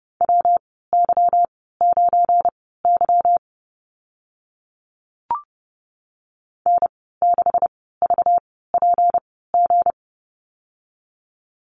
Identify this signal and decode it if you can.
Morse code, heard as a keyed tone, about 30 wpm, spaced as for 17 wpm, 700 Hz. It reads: WY9Y N6VPG
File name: Lesson 37g - Letter x - Call Signs 30wpm